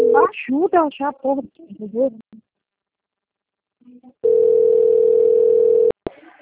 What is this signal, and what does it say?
Ringback tone (North America), 440 + 480 Hz, with speech.
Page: Americana 1